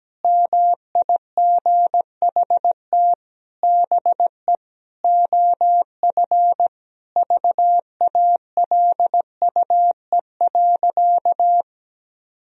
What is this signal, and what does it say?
Morse code, heard as a keyed tone, about 17 wpm, 700 Hz. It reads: MIGHT BE OF VALUE.
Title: Sentences from Top 1000 Words - Review Part 3 - 17wpm